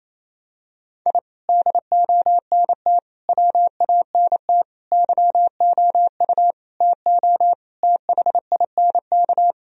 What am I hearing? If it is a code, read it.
Morse code, heard as a keyed tone, about 28 wpm, 700 Hz. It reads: I DONT WANT YOU TO THINK
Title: Sentences from Top 1000 Words - Review Part 3 - 28wpm